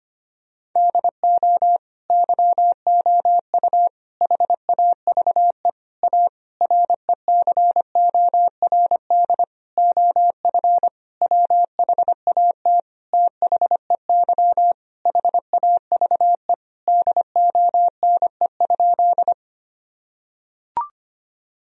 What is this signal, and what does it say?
Morse code, heard as a keyed tone, about 25 wpm, 700 Hz. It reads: DO YOU HAVE A RECORD OF WHAT THEY HAVE DONE?
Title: Sentences from Top 1000 Words - Review Part 3 - 25wpm